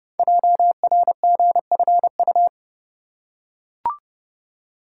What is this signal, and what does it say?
Morse code, heard as a keyed tone, about 30 wpm, 700 Hz. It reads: JRGFU
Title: Code Groups - 5 Letters - 30wpm